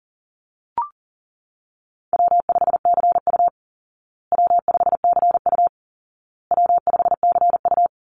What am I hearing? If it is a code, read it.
Morse code, heard as a keyed tone, about 40 wpm, 700 Hz. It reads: W5CU W5CU W5CU
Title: Easy Call Signs Repeated 3x - 40wpm